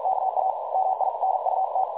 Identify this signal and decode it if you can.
Morse code, heard as a keyed tone, about 46 wpm, 760 Hz, with noise. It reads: KA MINUT
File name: dlazaawansowanych / nagraniazpasma / websdr recording 3565.0kHz